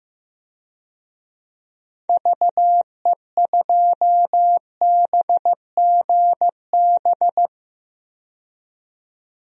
Morse code, heard as a keyed tone, about 15 wpm, 700 Hz. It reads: VE2BGB
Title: DXCC-100 Call Signs - 6 Characters - 15wpm